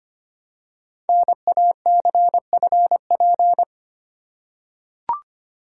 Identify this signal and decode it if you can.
Morse code, heard as a keyed tone, about 25 wpm, 700 Hz. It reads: NACFP
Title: Code Groups - 5 Letter:Numbers - 25wpm